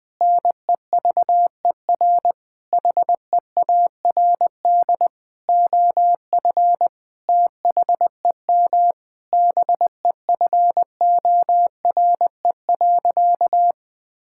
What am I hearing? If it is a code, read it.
Morse code, heard as a keyed tone, about 20 wpm, 700 Hz. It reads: NEVER HEARD OF THEM BEFORE.
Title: Sentences with 9-Word Phrases - 20wpm